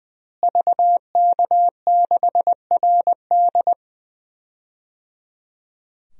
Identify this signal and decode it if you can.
Morse code, heard as a keyed tone, about 21 wpm, 700 Hz. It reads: VK6RD
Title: DXCC-100 Call Signs - 5 Characters - Repeated 3x - 20wpm